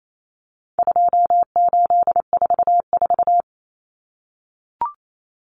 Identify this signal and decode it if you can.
Morse code, heard as a keyed tone, about 28 wpm, 700 Hz. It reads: 2844
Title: Numbers - 4-Digit 28wpm